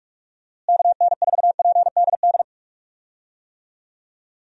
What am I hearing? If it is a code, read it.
Morse code, heard as a keyed tone, about 46 wpm, 700 Hz. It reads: KN4PDD
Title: USA Call Signs - 6 Characters - Repeated 2x - 45wpm